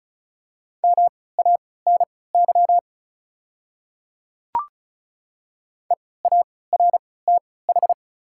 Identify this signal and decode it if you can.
Morse code, heard as a keyed tone, about 36 wpm, spaced as for 20 wpm, 700 Hz. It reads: MANY EARTH
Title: 3x Character Spacing - Top 200 Words - 35wpm